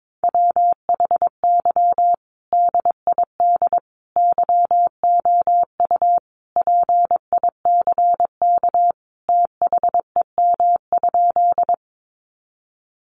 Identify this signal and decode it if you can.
Morse code, heard as a keyed tone, about 22 wpm, 700 Hz. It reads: WHY DID YOU PICK THEM?
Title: Sentences from Top 1000 Words 22wpm